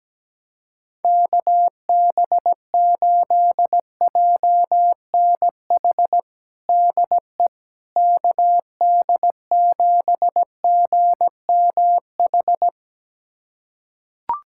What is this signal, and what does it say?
Morse code, heard as a keyed tone, about 17 wpm, 700 Hz. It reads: KB8JNH DE KD7GMH
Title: USA Callsign DE Callsign - 6 Characters - 17wpm